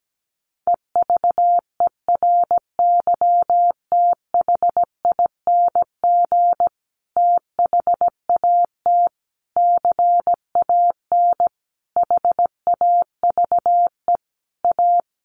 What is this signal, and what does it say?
Morse code, heard as a keyed tone, about 17 wpm, 700 Hz. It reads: EVERYTHING THAT CAN HAVE A
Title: Sentences from Top 1300 Words 17wpm